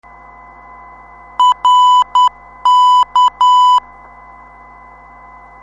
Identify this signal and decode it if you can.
Morse code, heard as a keyed tone, about 10 wpm, 1000 Hz. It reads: RK
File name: ndb zvok / rk 359khz 25mar2016 1447utc